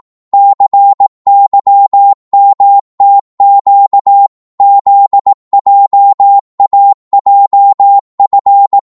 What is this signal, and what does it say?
Morse code, heard as a keyed tone, about 18 wpm, 800 Hz. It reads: CYMTQ ZJAJF
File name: Morse / ltr-18wpm